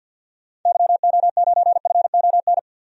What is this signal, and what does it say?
Morse code, heard as a keyed tone, about 47 wpm, 700 Hz. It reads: YO9FON
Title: DXCC-100 Call Signs - 6 Characters - 50wpm